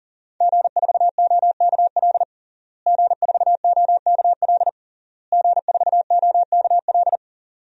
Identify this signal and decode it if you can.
Morse code, heard as a keyed tone, about 40 wpm, 700 Hz. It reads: G4OKL G4OKL G4OKL
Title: DXCC-100 Call Signs - 5 Characters - Repeated 3x - 40wpm